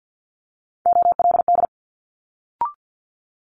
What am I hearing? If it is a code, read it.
Morse code, heard as a keyed tone, about 50 wpm, 700 Hz. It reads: OLD
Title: Multi-letter Phonemes Quick Fire 50wpm